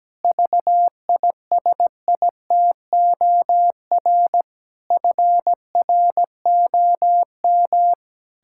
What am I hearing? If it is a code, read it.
Morse code, heard as a keyed tone, about 17 wpm, 700 Hz. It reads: VISITOR FROM
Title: Pangram Challenge v2 - 17wpm